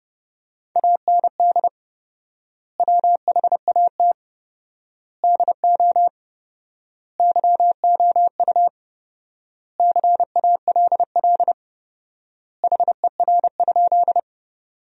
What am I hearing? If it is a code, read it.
Morse code, heard as a keyed tone, about 32 wpm, 700 Hz. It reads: AND WHAT DO YOU CALL HER?
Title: 4x Word-Spacing - Sentences from Top 100 Words - 30wpm